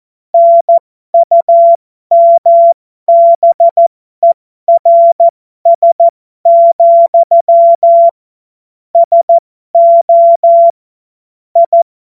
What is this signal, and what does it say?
Morse code, heard as a keyed tone, about 14 wpm, spaced as for 12 wpm, 680 Hz. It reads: NUMBERS, SO I
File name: Morsekurs / Witze